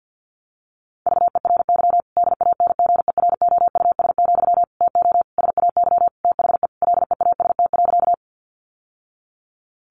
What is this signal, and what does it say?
Morse code, heard as a keyed tone, about 50 wpm, 700 Hz. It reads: VERY DANGEROUS, TO SAY THE LEAST.
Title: Sentences from Top 1200 Words 50wpm